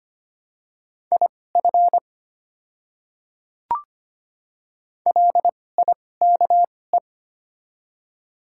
Morse code, heard as a keyed tone, about 25 wpm, spaced as for 18 wpm, 700 Hz. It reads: IF LIKE
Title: 2x Character Spacing - Top 100 Words - 25wpm